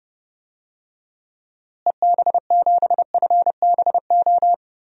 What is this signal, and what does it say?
Morse code, heard as a keyed tone, about 30 wpm, 700 Hz. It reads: EB7FBO
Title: DXCC-100 Call Signs - 6 Characters - Repeated 2x - 30wpm